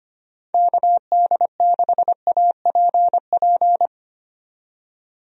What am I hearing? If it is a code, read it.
Morse code, heard as a keyed tone, about 25 wpm, 700 Hz. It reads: KD6APP
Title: USA Call Signs - 6 Characters - 25wpm